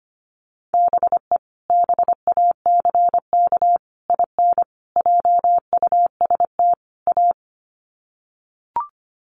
Morse code, heard as a keyed tone, about 25 wpm, 700 Hz. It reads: BE BACK IN JUST A
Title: Sets of 5 Words Encore 25wpm